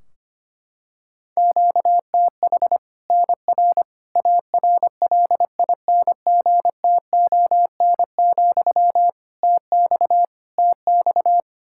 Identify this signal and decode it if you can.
Morse code, heard as a keyed tone, about 25 wpm, 700 Hz. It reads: QTH NR ARLINGTON, TX TX
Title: K1USN SST QSO Practice 25wpm